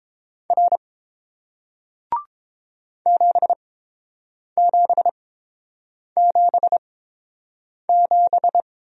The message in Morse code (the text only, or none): R 7777